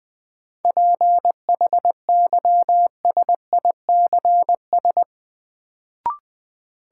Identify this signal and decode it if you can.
Morse code, heard as a keyed tone, about 21 wpm, 700 Hz. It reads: PHYSICS
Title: Sentences from Top 1500 Words - New Word Drill Alt - 20wpm